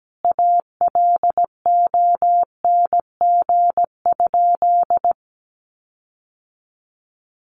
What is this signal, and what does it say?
Morse code, heard as a keyed tone, about 17 wpm, 700 Hz. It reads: ALONG?
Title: Sentences from Top 1300 Words 17wpm